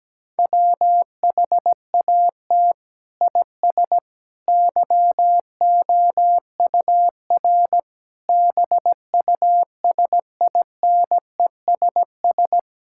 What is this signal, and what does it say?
Morse code, heard as a keyed tone, about 17 wpm, 700 Hz. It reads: WHAT IS YOUR BUSINESS
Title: Sentences from Top 1000 Words - Review Part 5 - 17wpm